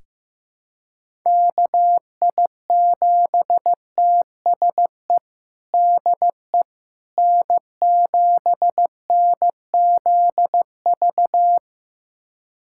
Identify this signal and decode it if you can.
Morse code, heard as a keyed tone, about 15 wpm, 700 Hz. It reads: KI7TSE DE N7NZV
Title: USA Callsign DE Callsign 15wpm